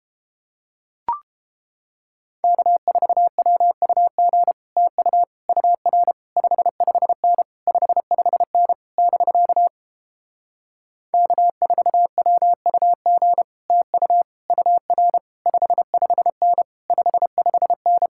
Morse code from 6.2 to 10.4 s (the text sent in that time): 55N 55N <BK>